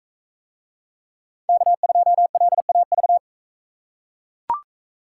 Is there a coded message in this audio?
Yes: Morse code from 1.5 to 3.2 s, heard as a keyed tone, about 42 wpm, 700 Hz. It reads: K2LAV